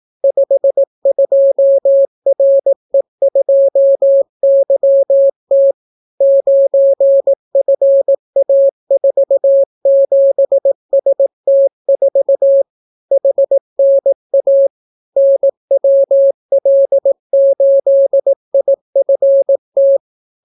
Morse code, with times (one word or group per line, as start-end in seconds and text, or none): 0.2-5.7 52RE2YT
6.2-12.6 9FA47ST4
13.1-14.7 HNA
15.2-20.0 NWL8IFT